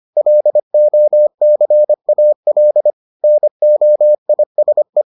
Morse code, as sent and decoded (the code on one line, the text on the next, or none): .-.. --- -.-. .- .-.. / -. --- .. ... .
LOCAL NOISE